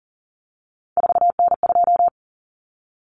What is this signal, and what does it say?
Morse code, heard as a keyed tone, about 40 wpm, 700 Hz. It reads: 4N2
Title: Common Cut Numbers - 3-Digit 40wpm